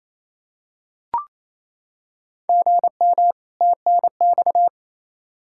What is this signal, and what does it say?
Morse code, heard as a keyed tone, about 28 wpm, 700 Hz. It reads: GM TNX
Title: Practice QSO - Exchange 1 - Beginning - 28wpm